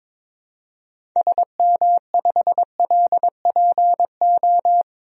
Morse code, heard as a keyed tone, about 21 wpm, 700 Hz. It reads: SM5LPO
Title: DXCC-100 Call Signs - 6 Characters - 22wpm